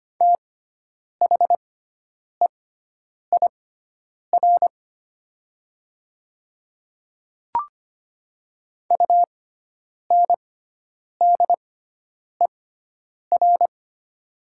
Morse code, heard as a keyed tone, about 25 wpm, spaced as for 9 wpm, 700 Hz. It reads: THEIR UNDER
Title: 6x Character Spacing - Top 300 Words - 25wpm